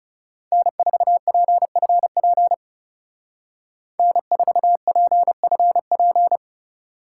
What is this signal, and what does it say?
Morse code, heard as a keyed tone, about 38 wpm, 700 Hz. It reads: N4PFP N4PFP